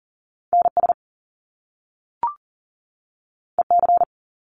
Morse code, heard as a keyed tone, about 38 wpm, 700 Hz. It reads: NS EC
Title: Top Two Letter Combinations in Words 40wpm